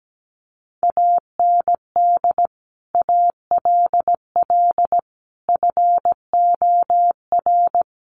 Morse code, heard as a keyed tone, about 17 wpm, 700 Hz. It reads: AND ALL FOR
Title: Sentences from Top 100 Words 17wpm